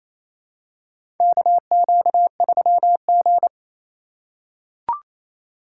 Morse code, heard as a keyed tone, about 28 wpm, 700 Hz. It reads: KQ3G
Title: USA Call Signs - 4 Characters - 3x Character Spacing - 28wpm